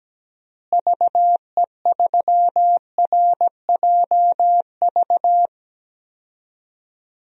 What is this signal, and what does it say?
Morse code, heard as a keyed tone, about 17 wpm, 700 Hz. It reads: VE3RJV
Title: DXCC-100 Call Signs - 6 Characters - Repeated 2x - 17wpm